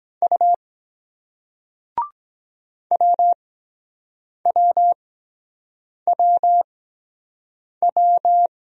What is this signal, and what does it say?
Morse code, heard as a keyed tone, about 23 wpm, spaced as for 7 wpm, 700 Hz. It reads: U WWWW